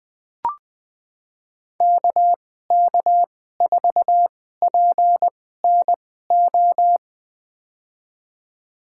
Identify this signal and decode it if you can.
Morse code, heard as a keyed tone, about 20 wpm, spaced as for 14 wpm, 700 Hz. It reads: KK4PNO
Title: USA Call Signs - 6 Characters - 2x Character Spacing - 20wpm